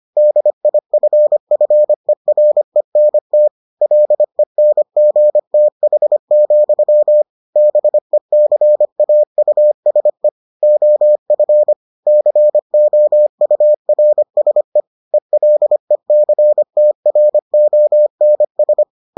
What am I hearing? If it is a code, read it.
Morse code, heard as a keyed tone, about 25 wpm, 600 Hz. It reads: DIFFERENT LENGTH, BECAUSE OF COURSE ELECTRONS